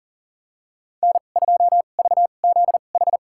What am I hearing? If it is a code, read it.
Morse code, heard as a keyed tone, about 40 wpm, spaced as for 29 wpm, 700 Hz. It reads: N2VZH
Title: USA Call Signs - 5 Characters - 2x Character Spacing - 40wpm